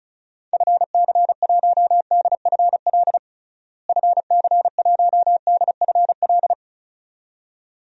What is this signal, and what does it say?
Morse code, heard as a keyed tone, about 34 wpm, 700 Hz. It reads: FC1DFL FC1DFL
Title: DXCC-100 Call Signs - 6 Characters - Repeated 2x - 35wpm